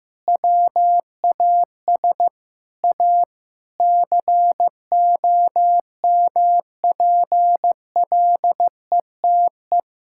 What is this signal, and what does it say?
Morse code, heard as a keyed tone, about 15 wpm, 700 Hz. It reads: WAS A COMPLETE
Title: Sentences from Top 1000 Words - Review Part 6 - 15wpm